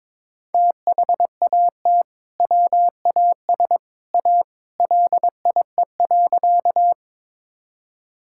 Morse code, heard as a keyed tone, about 22 wpm, 700 Hz. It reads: THAT WAS A LIE.
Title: Sentences from Top 1600 Words 22wpm